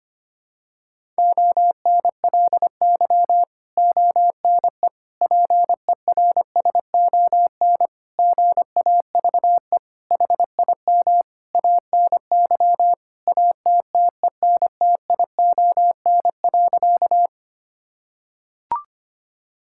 Morse code, heard as a keyed tone, about 25 wpm, 700 Hz. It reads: ONLY ONE PERSON GAVE HIM ANY ATTENTION.
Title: Sentences from Top 1000 Words - Review Part 6 - 25wpm